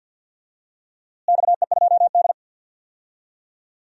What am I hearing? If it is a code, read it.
Morse code, heard as a keyed tone, about 51 wpm, 700 Hz. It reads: XE2D
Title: DXCC-100 Call Signs - 4 Characters - Repeated 3x - 50wpm